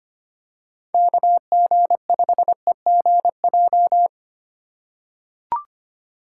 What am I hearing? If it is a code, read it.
Morse code, heard as a keyed tone, about 25 wpm, 700 Hz. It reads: KG5EGJ
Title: USA Call Signs - 3x Character Spacing - 25wpm